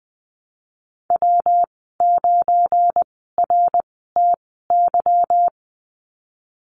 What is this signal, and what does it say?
Morse code, heard as a keyed tone, about 20 wpm, spaced as for 15 wpm, 700 Hz. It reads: W9RTY
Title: 186 Lesson 30g - Letter b - Call Signs 20wpm